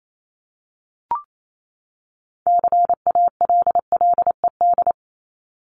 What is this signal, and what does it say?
Morse code, heard as a keyed tone, about 28 wpm, 700 Hz. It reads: CALLED
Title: ED Suffix - Top 100 Words with Base Words 28wpm